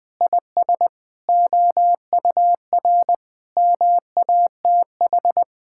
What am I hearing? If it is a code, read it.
Morse code, heard as a keyed tone, about 20 wpm, 700 Hz. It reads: IS OUR MATH
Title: Sentences from Top 1000 Words - Review Part 6 - 20wpm